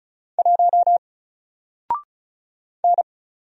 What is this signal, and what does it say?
Morse code, heard as a keyed tone, about 36 wpm, 700 Hz. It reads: 1 N
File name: Single-Letter-Number-35wpm